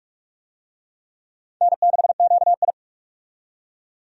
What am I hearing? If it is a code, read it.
Morse code, heard as a keyed tone, about 46 wpm, 700 Hz. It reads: N6QI
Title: USA Call Signs - 4 Characters - 45wpm